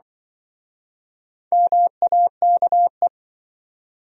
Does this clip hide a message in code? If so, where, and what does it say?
Yes: Morse code from 1.5 to 3.1 s, heard as a keyed tone, about 27 wpm, 700 Hz. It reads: MAKE